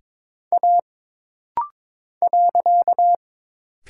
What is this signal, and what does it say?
Morse code, heard as a keyed tone, about 22 wpm, 700 Hz. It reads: A .
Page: This rapid-fire timing is geared to push your ICR (Instant Character Recognition) skills.